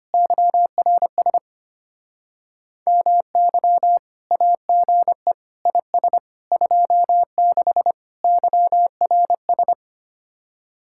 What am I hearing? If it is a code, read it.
Morse code, heard as a keyed tone, about 32 wpm, 700 Hz. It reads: YRS MY AGE IS 26 YRS